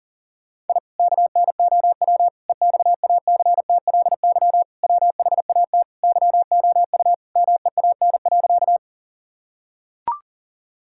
Morse code, heard as a keyed tone, about 40 wpm, 700 Hz. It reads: I KNOW EXACTLY WHAT YOU MEAN.
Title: Sentences from Top 1000 Words - Review Part 6 - 40wpm